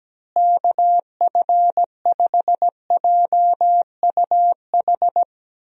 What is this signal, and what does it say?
Morse code, heard as a keyed tone, about 17 wpm, 700 Hz. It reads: KF5JUH
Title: USA Call Signs - 6 Characters - Repeated 2x - 17wpm